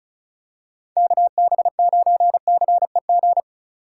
Morse code, heard as a keyed tone, about 36 wpm, 700 Hz. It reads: KB9CEG
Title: USA Call Signs - 6 Characters - 3x Character Spacing - 35wpm